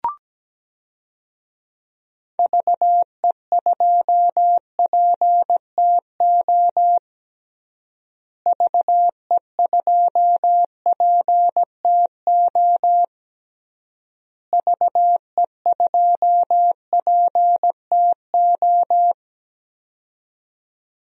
Morse code, heard as a keyed tone, about 17 wpm, 700 Hz. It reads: VE2PTO VE2PTO VE2PTO
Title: DXCC-100 Call Signs - 6 Characters - Repeated 3x - 17wpm